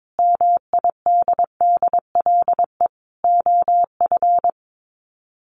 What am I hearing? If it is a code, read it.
Morse code, heard as a keyed tone, about 22 wpm, 700 Hz. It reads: MIDDLE OF
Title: Sets of 5 Words Encore 22wpm